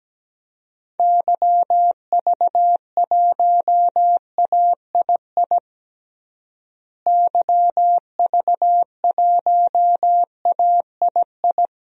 Morse code, heard as a keyed tone, about 17 wpm, 700 Hz. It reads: YV1AII YV1AII
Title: DXCC-100 Call Signs - 6 Characters - Repeated 2x - 17wpm